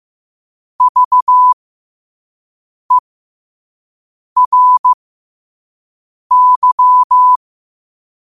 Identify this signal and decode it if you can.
Morse code, heard as a keyed tone, about 15 wpm, spaced as for 5 wpm, 1000 Hz. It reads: VERY
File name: qotdfiles / 5wpm